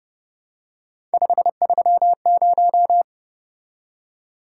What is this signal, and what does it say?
Morse code, heard as a keyed tone, about 29 wpm, 700 Hz. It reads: 530